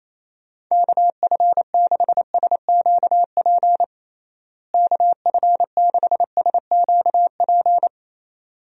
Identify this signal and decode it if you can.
Morse code, heard as a keyed tone, about 28 wpm, 700 Hz. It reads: KF6SQP KF6SQP
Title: USA Call Signs - 6 Characters - Repeated 2x - 28wpm